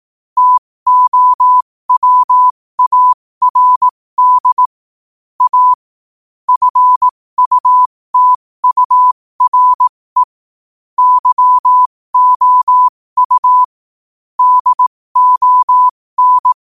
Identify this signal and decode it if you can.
Morse code, heard as a keyed tone, about 18 wpm, spaced as for 15 wpm, 1000 Hz. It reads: TOWARD A FUTURE YOU DON
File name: qotdfiles / 15wpm